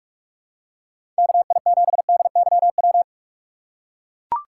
Morse code, heard as a keyed tone, about 43 wpm, 700 Hz. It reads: KI7DYW
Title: USA Call Signs - 4x Character Spacing - 45wpm